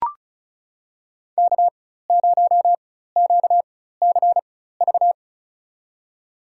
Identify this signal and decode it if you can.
Morse code, heard as a keyed tone, about 34 wpm, spaced as for 16 wpm, 700 Hz. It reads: K0QCV
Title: USA Call Signs - 5 Characters - 4x Character Spacing - 35wpm